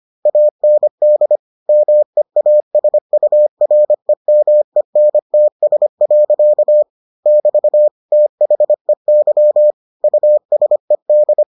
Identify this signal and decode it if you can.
Morse code, heard as a keyed tone, about 25 wpm, 600 Hz. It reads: AND MEASUREMENTS. <BT> THEY USED